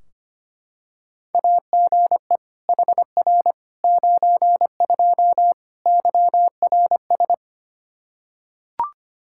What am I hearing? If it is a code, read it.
Morse code, heard as a keyed tone, about 25 wpm, 700 Hz. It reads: AGE HR 92 YRS
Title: Practice QSO - Exchange 3 - More Info - 25wpm